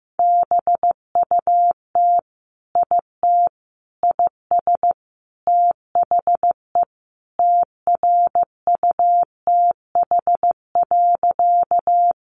Morse code, heard as a keyed tone, about 15 wpm, 700 Hz. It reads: BUT IT IS THE TRUTH.
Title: Sentences from Top 800 Words 15wpm